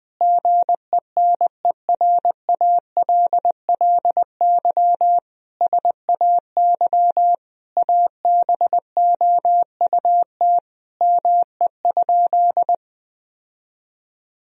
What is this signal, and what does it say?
Morse code, heard as a keyed tone, about 20 wpm, 700 Hz. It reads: GENERALLY SAY ABOUT ME?
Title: Sentences from Top 1000 Words - Review Part 4 - 20wpm